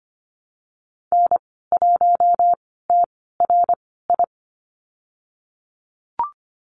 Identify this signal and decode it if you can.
Morse code, heard as a keyed tone, about 25 wpm, spaced as for 16 wpm, 700 Hz. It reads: N1TRI
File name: Lesson 15g - Number 5 - Call Signs 25wpm